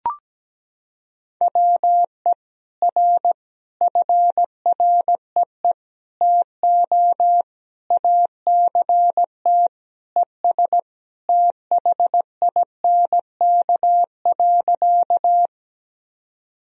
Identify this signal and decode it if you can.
Morse code, heard as a keyed tone, about 17 wpm, 700 Hz. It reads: WE R FREE TO ACT ES THINK.
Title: Sentences from Top 500 Words and Abbreviations 17wpm